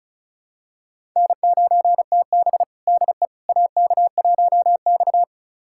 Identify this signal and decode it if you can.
Morse code, heard as a keyed tone, about 36 wpm, 700 Hz. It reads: N9TB DE AK1X
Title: USA Callsign DE Callsign - 4 Characters - 35wpm